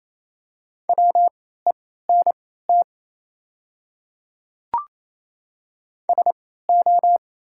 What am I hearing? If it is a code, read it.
Morse code, heard as a keyed tone, about 28 wpm, spaced as for 16 wpm, 700 Hz. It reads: WENT SO
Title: 3x Character Spacing - Top 300 Words - 28wpm